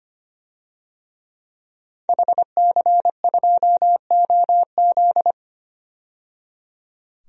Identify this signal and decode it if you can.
Morse code, heard as a keyed tone, about 25 wpm, 700 Hz. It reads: HC2OZ
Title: DXCC-100 Call Signs - 5 Characters - 25wpm